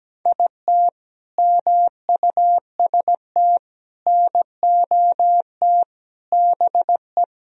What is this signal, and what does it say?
Morse code, heard as a keyed tone, about 17 wpm, 700 Hz. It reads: IT MUST NOT BE
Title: Sentences from Top 1000 Words - Review Part 1 - 17wpm